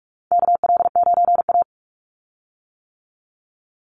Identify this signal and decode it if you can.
Morse code, heard as a keyed tone, about 44 wpm, 700 Hz. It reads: KL9A
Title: Call Signs - 45wpm